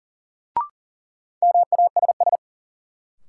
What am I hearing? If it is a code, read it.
Morse code, heard as a keyed tone, about 40 wpm, 700 Hz. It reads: MASS
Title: Sets of 5 Words Encore - New Word Drill - 40wpm